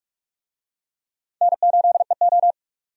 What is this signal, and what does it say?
Morse code, heard as a keyed tone, about 46 wpm, 700 Hz. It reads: N8EO
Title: USA Call Signs - 4 Characters - 45wpm